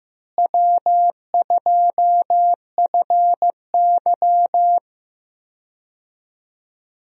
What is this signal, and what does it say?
Morse code, heard as a keyed tone, about 15 wpm, 700 Hz. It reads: W2FY
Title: USA Call Signs - 4 Characters - Repeated 3x - 15wpm